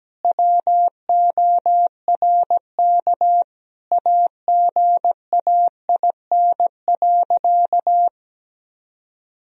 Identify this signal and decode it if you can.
Morse code, heard as a keyed tone, about 17 wpm, 700 Hz. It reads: WORK AGAIN.
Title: Sentences from Top 1000 Words - Review Part 2 - 17wpm